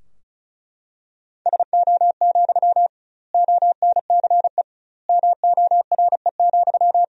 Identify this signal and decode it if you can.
Morse code, heard as a keyed tone, about 36 wpm, 700 Hz. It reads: SO, ONCE MORE,
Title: 2x Word-Spacing - Sentences from Top 300 Words - 35wpm